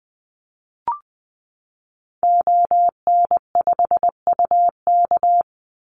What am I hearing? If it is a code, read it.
Morse code, heard as a keyed tone, about 20 wpm, 700 Hz. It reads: ON5UK
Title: Call Signs - 20wpm